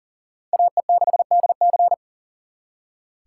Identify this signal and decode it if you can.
Morse code, heard as a keyed tone, about 52 wpm, 700 Hz. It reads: AE6DC